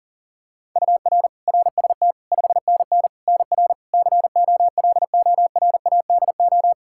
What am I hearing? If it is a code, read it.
Morse code, heard as a keyed tone, about 40 wpm, 700 Hz. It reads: UR RST 5NN NR COLORADO
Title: Practice QSO - Exchange 1 - Beginning - 40wpm